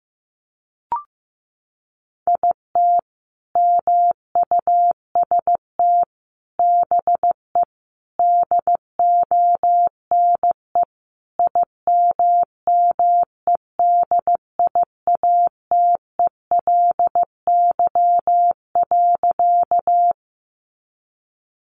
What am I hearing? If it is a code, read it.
Morse code, heard as a keyed tone, about 15 wpm, 700 Hz. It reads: IT MUST BE DONE IMMEDIATELY.
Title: Sentences from Top 1100 Words 15wpm